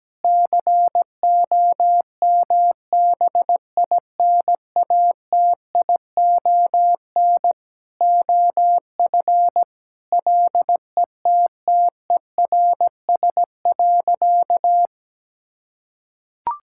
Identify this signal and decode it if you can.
Morse code, heard as a keyed tone, about 17 wpm, 700 Hz. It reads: COMBINATION OF LETTERS.
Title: Sentences from Top 1000 Words - Review Part 6 - 17wpm